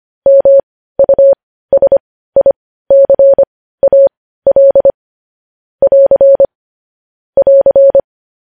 Morse code, heard as a keyed tone, about 25 wpm, spaced as for 15 wpm, 550 Hz. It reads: MUSICAL <AR> <AR>